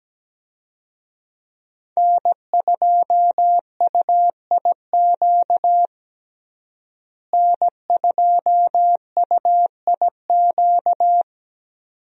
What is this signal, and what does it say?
Morse code, heard as a keyed tone, about 17 wpm, 700 Hz. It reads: N2UIQ N2UIQ
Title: USA Call Signs - 5 Characters - Repeated 2x - 17wpm